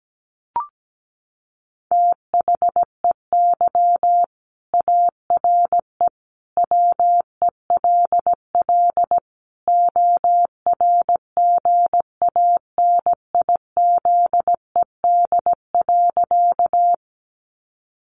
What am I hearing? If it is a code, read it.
Morse code, heard as a keyed tone, about 17 wpm, 700 Hz. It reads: THEY ARE WELL ORGANIZED.
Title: Sentences from Top 1300 Words 17wpm